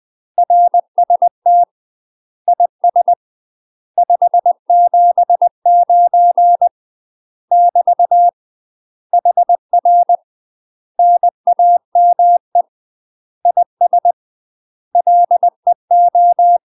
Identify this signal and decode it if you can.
Morse code, heard as a keyed tone, about 20 wpm, 700 Hz. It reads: RST IS 579 <BT> HR NAME IS LEO
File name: UK MASTER QSOs TRAINING/20WPM